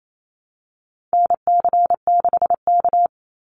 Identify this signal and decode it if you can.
Morse code, heard as a keyed tone, about 28 wpm, 700 Hz. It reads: NC6K
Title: Call Signs - 28wpm